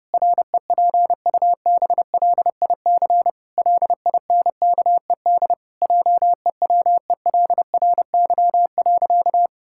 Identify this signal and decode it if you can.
Morse code, heard as a keyed tone, about 32 wpm, 700 Hz. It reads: REPUBLIC LINKED JEWELRY.
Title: Pangram Challenge v2 - 30wpm